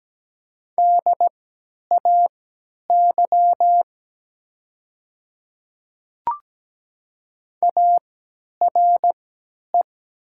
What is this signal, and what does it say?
Morse code, heard as a keyed tone, about 17 wpm, spaced as for 10 wpm, 700 Hz. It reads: DAY ARE
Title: 3x Character Spacing - Top 100 Words - 17wpm